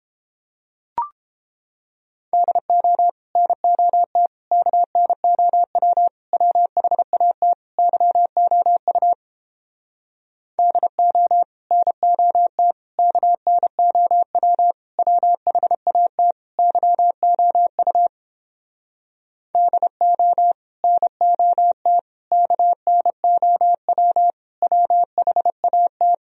Morse code, heard as a keyed tone, about 31 wpm, 700 Hz. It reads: DO NOT KNOW WHAT YOU DO NOT KNOW WHAT YOU DO NOT KNOW WHAT